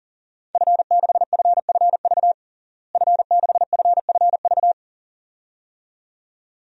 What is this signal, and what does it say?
Morse code, heard as a keyed tone, about 40 wpm, 700 Hz. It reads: F6FFV F6FFV
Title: DXCC-100 Call Signs - 5 Characters - Repeated 2x - 40wpm